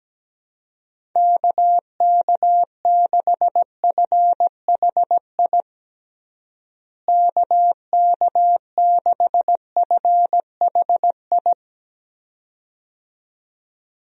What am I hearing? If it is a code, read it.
Morse code, heard as a keyed tone, about 17 wpm, 700 Hz. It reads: KK6FHI KK6FHI
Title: USA Call Signs - 6 Characters - Repeated 2x - 17wpm